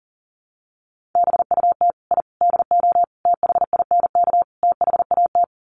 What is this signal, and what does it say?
Morse code, heard as a keyed tone, about 40 wpm, 700 Hz. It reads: BUT I DO THINK THAT
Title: Sets of 5 Words Encore2 40wpm